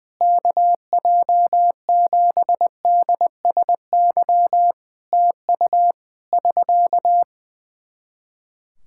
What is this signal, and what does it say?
Morse code, heard as a keyed tone, about 20 wpm, 700 Hz. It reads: KJ7DSY TU <SK>
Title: K1USN SST QSO Practice 20wpm